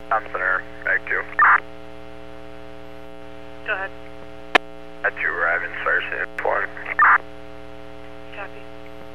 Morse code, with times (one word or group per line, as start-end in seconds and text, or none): none